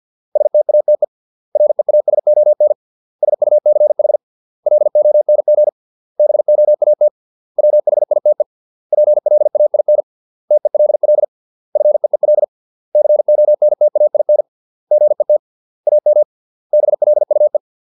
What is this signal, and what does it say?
Morse code, heard as a keyed tone, about 50 wpm, 600 Hz. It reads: STATE REASON SUCH LONG BOAT WHITE PLAIN TELL FEEL CONTAIN GET AM BLUE